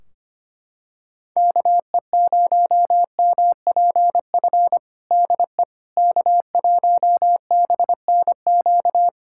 Morse code, heard as a keyed tone, about 25 wpm, 700 Hz. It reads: KE0MPF DE K1BNQ
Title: Practice QSO - Exchange 4 - Ending - 25wpm